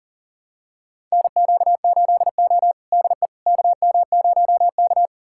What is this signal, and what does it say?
Morse code, heard as a keyed tone, about 40 wpm, 700 Hz. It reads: NQ8O DE KM0K
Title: USA Callsign DE Callsign - 4 Characters - 40wpm